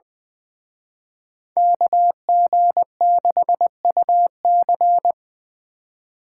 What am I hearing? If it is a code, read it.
Morse code, heard as a keyed tone, about 23 wpm, 700 Hz. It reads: KG6UC
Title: Speed Racing - USA 2x2 Call Signs 20wpm